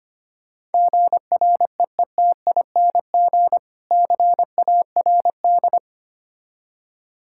Morse code, heard as a keyed tone, about 25 wpm, 700 Hz. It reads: GREETING CARD
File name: practice / 25wpm / Difficult Words with Hints - 25wpm